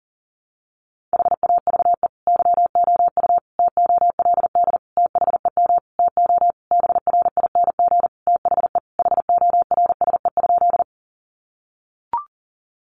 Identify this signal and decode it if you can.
Morse code, heard as a keyed tone, about 40 wpm, 700 Hz. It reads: HAVE YOU TOLD THEM TO BRING THE HORSE?
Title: Sentences from Top 500 Words 40wpm